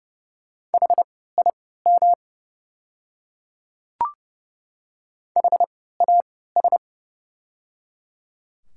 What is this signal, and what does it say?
Morse code, heard as a keyed tone, about 30 wpm, spaced as for 17 wpm, 700 Hz. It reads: HIM HAS
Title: 3x Character Spacing - Top 100 Words - 30wpm